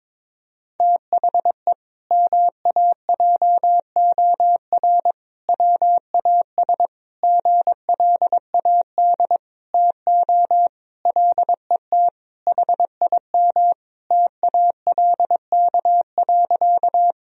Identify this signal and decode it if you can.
Morse code, heard as a keyed tone, about 22 wpm, 700 Hz. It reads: THE MAJOR WAS GLAD TO LET HIM TALK.
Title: Sentences from Top 1000 Words - Review Part 4 - 22wpm